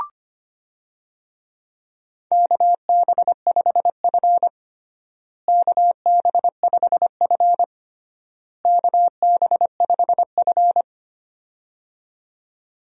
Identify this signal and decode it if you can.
Morse code, heard as a keyed tone, about 25 wpm, 700 Hz. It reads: KB5F KB5F KB5F
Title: USA Call Signs - 4 Characters - Repeated 3x - 25wpm